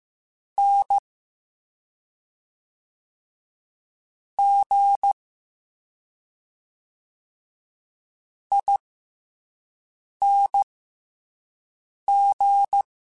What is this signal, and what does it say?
Morse code, heard as a keyed tone, about 15 wpm, spaced as for 5 wpm, 800 Hz. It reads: N G ING